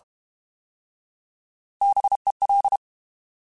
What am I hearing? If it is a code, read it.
Morse code, heard as a keyed tone, about 32 wpm, 800 Hz. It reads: BEL
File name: Alphabetical / Alphabetical mp3s / Bs @30